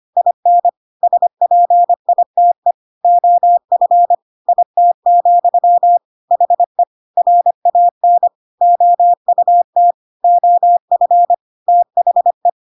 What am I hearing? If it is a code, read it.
Morse code, heard as a keyed tone, about 25 wpm, 700 Hz. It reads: IN SPITE OF IT, HE RAN OUT OF THE
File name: Pinocchio / Pinocchio ch6 25 25 ik1hge